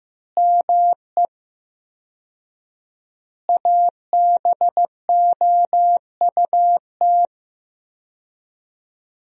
Morse code, heard as a keyed tone, about 15 wpm, 700 Hz. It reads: ME ABOUT
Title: 4x Word-Spacing - Sentences from Top 200 Words - 15wpm